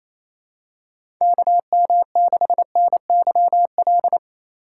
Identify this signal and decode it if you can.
Morse code, heard as a keyed tone, about 29 wpm, 700 Hz. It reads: KM6NYL
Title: USA Call Signs - 6 Characters - 4x Character Spacing - 28wpm